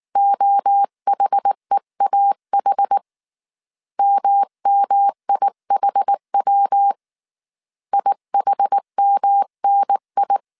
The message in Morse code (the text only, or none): OHEAH MMIHW IHMNI